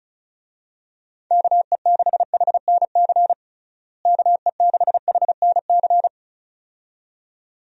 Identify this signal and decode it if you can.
Morse code, heard as a keyed tone, about 36 wpm, 700 Hz. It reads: KE6HNC KE6HNC
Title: USA Call Signs - 6 Characters - Repeated 2x - 35wpm